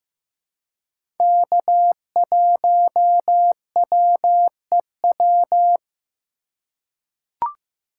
Morse code, heard as a keyed tone, about 15 wpm, 700 Hz. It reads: K1WEW
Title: USA Call Signs - 4x Character Spacing - 15wpm